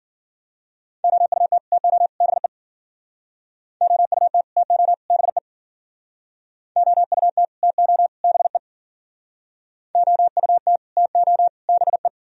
Morse code, heard as a keyed tone, about 53 wpm, 700 Hz. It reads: OUT TO BE OUT TO BE OUT TO BE OUT TO BE